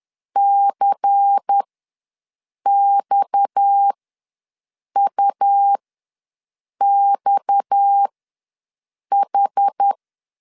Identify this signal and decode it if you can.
Morse code, heard as a keyed tone, about 10 wpm, spaced as for 6 wpm, 800 Hz. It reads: CXUXH
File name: LEZIONI / 6 PKX0 / 6 PKX ws 10wpm 3706